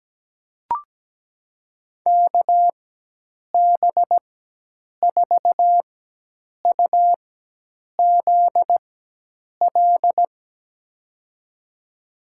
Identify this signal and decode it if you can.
Morse code, heard as a keyed tone, about 17 wpm, spaced as for 8 wpm, 700 Hz. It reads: KB4UZL
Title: USA Call Signs - 6 Characters - 4x Character Spacing - 17wpm